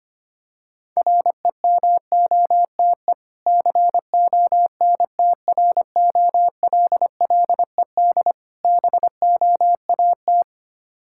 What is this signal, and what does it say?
Morse code, heard as a keyed tone, about 25 wpm, 700 Hz. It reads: REMOTE CONTROLLED BOAT
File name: practice / 25wpm / Difficult Words with Hints - 25wpm